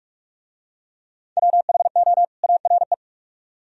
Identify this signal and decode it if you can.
Morse code, heard as a keyed tone, about 41 wpm, 700 Hz. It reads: WHO ARE